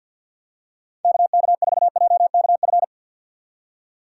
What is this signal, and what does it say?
Morse code, heard as a keyed tone, about 52 wpm, 700 Hz. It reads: KK4JKF
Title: USA Call Signs - 6 Characters - 6x Character Spacing - 50wpm